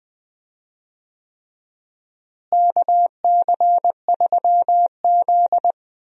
Morse code, heard as a keyed tone, about 20 wpm, 700 Hz. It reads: KC3Z
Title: USA Call Signs - 4 Characters - Repeated 3x - 20wpm